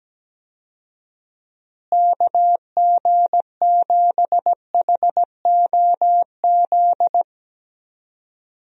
Morse code, heard as a keyed tone, about 17 wpm, 700 Hz. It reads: KG7HOZ
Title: USA Call Signs - 6 Characters - Repeated 3x - 17wpm